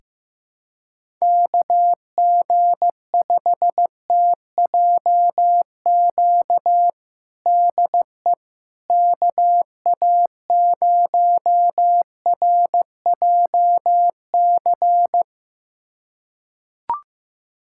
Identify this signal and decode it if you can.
Morse code, heard as a keyed tone, about 15 wpm, 700 Hz. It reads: KG5TJQ DE KA0RJC
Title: USA Callsign DE Callsign - 6 Characters - 15wpm